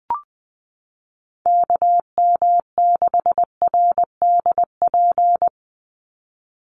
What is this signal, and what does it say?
Morse code, heard as a keyed tone, about 20 wpm, 700 Hz. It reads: KM6RDP
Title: USA Call Signs 20wpm